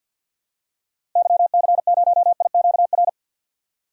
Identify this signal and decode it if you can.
Morse code, heard as a keyed tone, about 51 wpm, 700 Hz. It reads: YC0IQR
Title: DXCC-100 Call Signs - 6 Characters - Repeated 2x - 50wpm